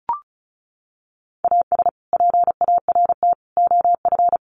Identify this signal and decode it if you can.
Morse code, heard as a keyed tone, about 34 wpm, 700 Hz. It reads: AS PART OF
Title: Sets of 3 Words 35wpm